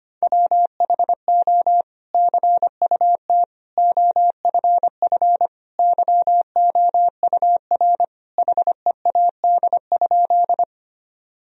Morse code, heard as a keyed tone, about 25 wpm, 700 Hz. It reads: WHO CUT OFF YOUR HEAD?
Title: Sentences from Top 1000 Words - Review Part 1 - 25wpm